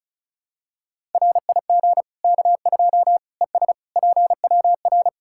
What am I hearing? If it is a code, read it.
Morse code, heard as a keyed tone, about 36 wpm, 700 Hz. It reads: RIG K2 ES PWR
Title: Practice QSO - Exchange 2 - Info - 35wpm